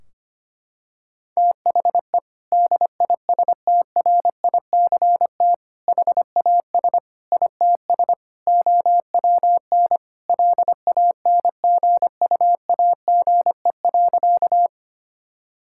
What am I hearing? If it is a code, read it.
Morse code, heard as a keyed tone, about 25 wpm, 700 Hz. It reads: THE DISTRICT HAS ITS OWN LANGUAGE.
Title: Sentences from Top 1600 Words 25wpm